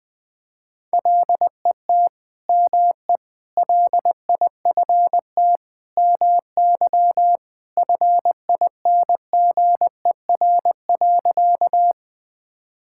Morse code, heard as a keyed tone, about 20 wpm, 700 Hz. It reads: LET ME LIFT MY FINGER.
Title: Sentences from Top 1600 Words 20wpm